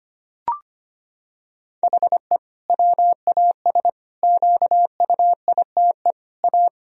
Morse code, heard as a keyed tone, about 25 wpm, 700 Hz. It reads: HE WAS QUITE A
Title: Sentences from Top 1000 Words - Review Part 3 - 25wpm